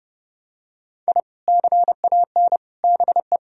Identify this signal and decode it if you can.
Morse code, heard as a keyed tone, about 30 wpm, 700 Hz. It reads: I CAN BE
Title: Sentences from Top 1000 Words - Review Part 3 - 30wpm